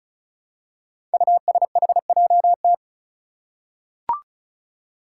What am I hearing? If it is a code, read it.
Morse code, heard as a keyed tone, about 34 wpm, 700 Hz. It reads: USHJT
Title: Five Challenging Letter-Number Combinations 35wpm